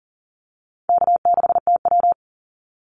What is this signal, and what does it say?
Morse code, heard as a keyed tone, about 40 wpm, 700 Hz. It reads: K6TW
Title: NA SOTA Activator Callsigns 40wpm